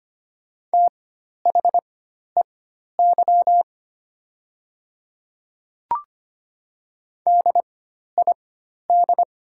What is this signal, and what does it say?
Morse code, heard as a keyed tone, about 25 wpm, spaced as for 12 wpm, 700 Hz. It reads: THEY DID
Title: 4x Character Spacing - Top 100 Words - 25wpm